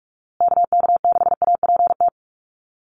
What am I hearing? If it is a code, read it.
Morse code, heard as a keyed tone, about 44 wpm, 700 Hz. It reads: KK6APT
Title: USA Call Signs Repeated 2x 45wpm